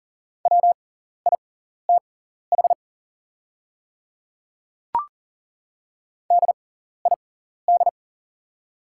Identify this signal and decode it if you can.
Morse code, heard as a keyed tone, about 41 wpm, spaced as for 14 wpm, 700 Hz. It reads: WITH DID
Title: 6x Character Spacing - Top 200 Words - 40wpm